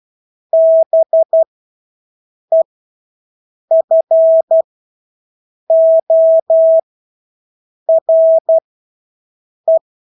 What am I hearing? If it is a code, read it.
Morse code, heard as a keyed tone, about 12 wpm, spaced as for 6 wpm, 650 Hz. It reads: BEFORE
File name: Engelske ord 12-6 wpm